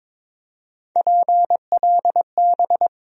Morse code, heard as a keyed tone, about 22 wpm, 700 Hz. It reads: PLB
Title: USA Call Sign Postfixes - 3 Letters 22wpm